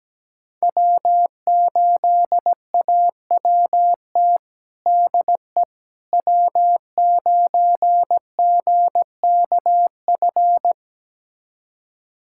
Morse code, heard as a keyed tone, about 17 wpm, 700 Hz. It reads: W8AWT DE W9GKF
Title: USA Callsign DE Callsign - 5 Characters - 17wpm